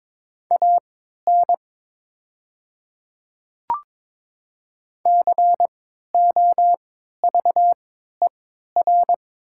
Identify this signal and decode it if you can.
Morse code, heard as a keyed tone, about 22 wpm, spaced as for 12 wpm, 690 Hz. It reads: AN COVER
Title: 3x Character Spacing - Top 300 Words - 22wpm